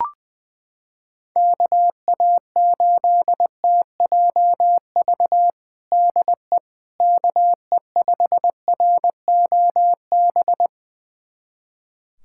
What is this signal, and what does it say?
Morse code, heard as a keyed tone, about 21 wpm, 700 Hz. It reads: KA8TJV DE KE5ROB
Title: USA Callsign DE Callsign 20wpm